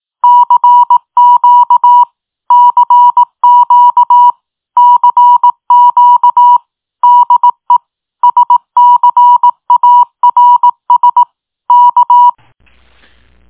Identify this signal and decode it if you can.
Morse code, heard as a keyed tone, about 18 wpm, 970 Hz. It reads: CQ CQ CQ DE SCARS K
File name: tin-upgrade / QSO 2022 06 16 .mp3